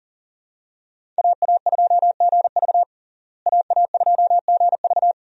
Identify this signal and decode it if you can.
Morse code, heard as a keyed tone, about 40 wpm, 700 Hz. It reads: AA2GV AA2GV
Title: USA Call Signs - 5 Characters - Repeated 2x - 40wpm